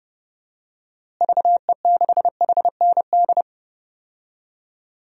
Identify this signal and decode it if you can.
Morse code, heard as a keyed tone, about 30 wpm, 700 Hz. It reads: VE6HND
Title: DXCC-100 Call Signs - 6 Characters - Repeated 2x - 30wpm